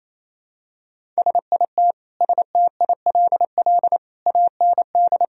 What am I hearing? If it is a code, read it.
Morse code, heard as a keyed tone, about 28 wpm, 700 Hz. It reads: SIT STILL AND
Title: Sentences from Top 1000 Words - Review Part 1 - 28wpm